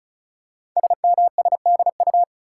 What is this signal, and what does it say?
Morse code, heard as a keyed tone, about 36 wpm, 700 Hz. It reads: SMSDU
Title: Code Groups - 5 Letters - 35wpm